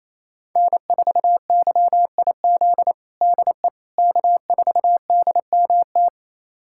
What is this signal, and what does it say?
Morse code, heard as a keyed tone, about 27 wpm, 700 Hz. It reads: N4YIZ DE K4DMT
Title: USA Callsign DE Callsign - 5 Characters - 28wpm